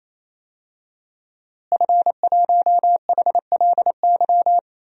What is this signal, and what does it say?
Morse code, heard as a keyed tone, about 28 wpm, 700 Hz. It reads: F1HLY
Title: DXCC-100 Call Signs - 5 Characters - Repeated 3x - 28wpm